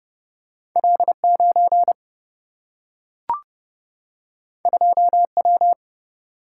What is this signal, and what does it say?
Morse code, heard as a keyed tone, about 30 wpm, 700 Hz. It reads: L 9 2 W
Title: Single-Letter Single-Number 30wpm